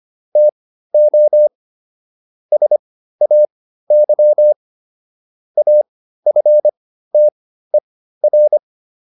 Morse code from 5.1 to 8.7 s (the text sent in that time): AFTER